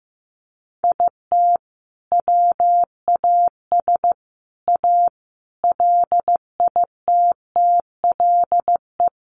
Morse code, heard as a keyed tone, about 15 wpm, 700 Hz. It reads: IT WAS A LITTLE
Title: Sentences from Top 1500 Words 15wpm